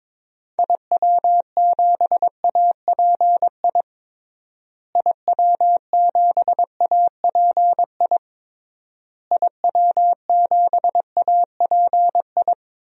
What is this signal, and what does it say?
Morse code, heard as a keyed tone, about 22 wpm, 700 Hz. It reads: IW7API IW7API IW7API
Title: DXCC-100 Call Signs - 6 Characters - Repeated 3x - 22wpm